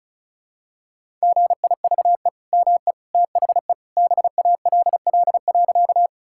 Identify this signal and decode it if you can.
Morse code, heard as a keyed tone, about 36 wpm, 700 Hz. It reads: GIVE ME THE BALL.
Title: Sentences from Top 1000 Words - Review Part 6 - 35wpm